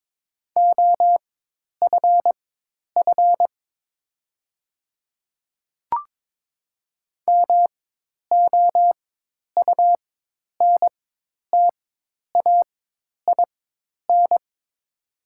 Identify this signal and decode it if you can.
Morse code, heard as a keyed tone, about 22 wpm, spaced as for 10 wpm, 690 Hz. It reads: OFF MOUNTAIN
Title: 4x Character Spacing - Top 300 Words - 22wpm